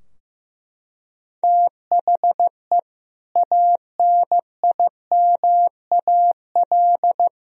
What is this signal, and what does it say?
Morse code, heard as a keyed tone, about 16 wpm, 700 Hz. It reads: THE ANIMAL
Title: Sentences from Top 1000 Words - Review Part 3 - 15wpm